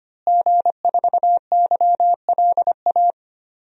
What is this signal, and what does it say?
Morse code, heard as a keyed tone, about 25 wpm, 700 Hz. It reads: G4YLA
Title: DXCC-100 Call Signs - 5 Characters - Repeated 3x - 25wpm